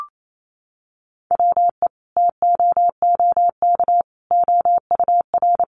Morse code, heard as a keyed tone, about 28 wpm, 700 Hz. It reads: WE TOOK OUR
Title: Sentences from Top 400 Words 28wpm